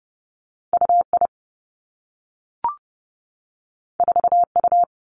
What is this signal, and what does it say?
Morse code, heard as a keyed tone, about 30 wpm, 700 Hz. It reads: U I 4 U
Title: Two Challenging Letter-Number Combinations 30wpm